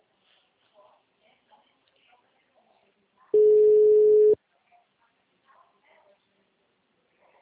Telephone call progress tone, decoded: ringback tone

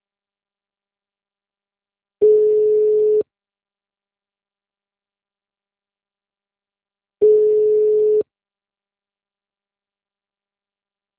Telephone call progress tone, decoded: ringback tone

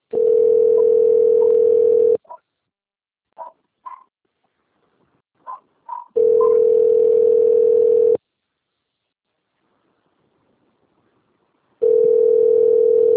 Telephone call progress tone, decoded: ringback tone

